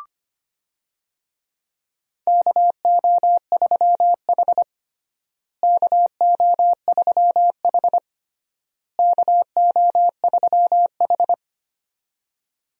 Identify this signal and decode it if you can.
Morse code, heard as a keyed tone, about 25 wpm, 700 Hz. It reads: KO3H KO3H KO3H
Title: USA Call Signs - 4 Characters - Repeated 3x - 25wpm